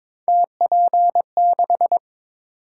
Morse code, heard as a keyed tone, about 22 wpm, 700 Hz. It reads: TP6
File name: practice / 22wpm / Three Letter-Number Combinations - 22wpm